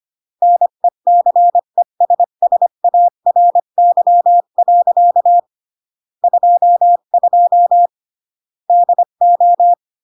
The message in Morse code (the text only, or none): NECESSARY. 22 DO